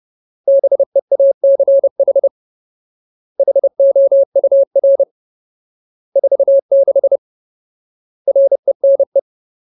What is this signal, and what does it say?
Morse code, heard as a keyed tone, about 30 wpm, 550 Hz. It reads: BEACH HOUR 46 RENE
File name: Ultimate Word List / 30WPM Ultimate